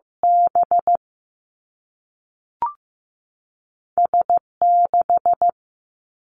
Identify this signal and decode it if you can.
Morse code, heard as a keyed tone, about 15 wpm, spaced as for 3 wpm, 700 Hz. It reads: BS 6
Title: Two Challenging Letter-Number Combinations 15wpm